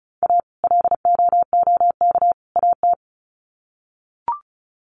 Morse code, heard as a keyed tone, about 34 wpm, 700 Hz. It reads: A LOOK AT
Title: Sets of 3 Words 35wpm